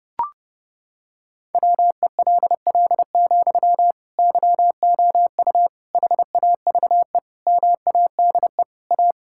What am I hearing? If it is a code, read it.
Morse code, heard as a keyed tone, about 32 wpm, 700 Hz. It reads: WELL, YOU HAVE MADE A
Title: Sentences from Top 1000 Words - Review Part 6 - 30wpm